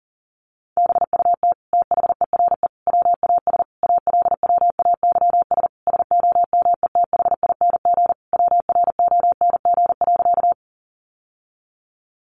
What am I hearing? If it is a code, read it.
Morse code, heard as a keyed tone, about 40 wpm, 700 Hz. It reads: BUT THERE WAS ALWAYS SOMETHING WRONG.
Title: Sentences from Top 900 Words 40wpm